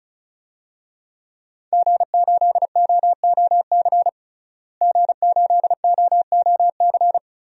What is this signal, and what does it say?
Morse code, heard as a keyed tone, about 36 wpm, 700 Hz. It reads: G8OOC G8OOC
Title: DXCC-100 Call Signs - 5 Characters - Repeated 3x - 35wpm